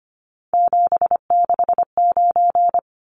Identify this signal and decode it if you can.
Morse code, heard as a keyed tone, about 25 wpm, 700 Hz. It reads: 769
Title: Numbers - 3-Digit 25wpm